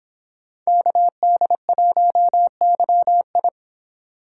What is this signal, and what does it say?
Morse code, heard as a keyed tone, about 23 wpm, 700 Hz. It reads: KD1YI